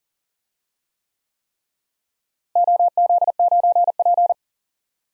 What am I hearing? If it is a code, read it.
Morse code, heard as a keyed tone, about 40 wpm, 700 Hz. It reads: OZ9P
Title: DXCC-100 Call Signs - 4 Characters - Repeated 3x - 40wpm